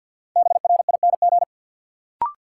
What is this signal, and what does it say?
Morse code, heard as a keyed tone, about 50 wpm, 700 Hz. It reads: BRING
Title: Sets of 5 Words Encore - New Word Drill - 50wpm